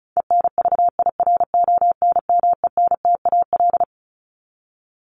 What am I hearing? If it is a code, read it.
Morse code, heard as a keyed tone, about 34 wpm, 700 Hz. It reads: ENVIRONMENTAL
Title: Top 1001-1250 Words 35wpm